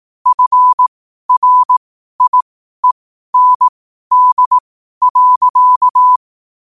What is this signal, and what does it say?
Morse code, heard as a keyed tone, about 18 wpm, spaced as for 12 wpm, 1000 Hz. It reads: FRIEND.
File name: qotdfiles / 12wpm